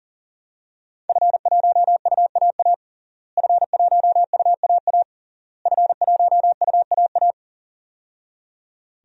Morse code, heard as a keyed tone, about 40 wpm, 700 Hz. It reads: F1UAA F1UAA F1UAA
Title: DXCC-100 Call Signs - 5 Characters - Repeated 3x - 40wpm